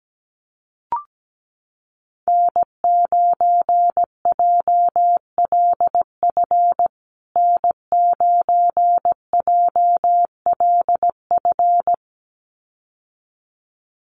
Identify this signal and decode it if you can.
Morse code, heard as a keyed tone, about 17 wpm, 700 Hz. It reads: N9JLF N9JLF
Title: USA Call Signs Repeated 2x 17wpm